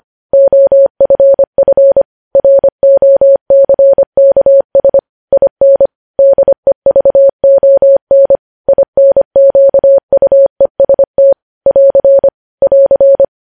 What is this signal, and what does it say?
Morse code, heard as a keyed tone, about 25 wpm, 550 Hz. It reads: OFF ROCKS IN DEVON INQUEST <AR> <AR>
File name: podcast / News headlines-1x-25x25